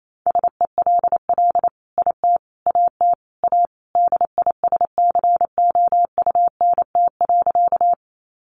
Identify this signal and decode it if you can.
Morse code, heard as a keyed tone, about 28 wpm, 700 Hz. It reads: SELL IT AT A DISCOUNT.
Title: Sentences from Top 1200 Words 28wpm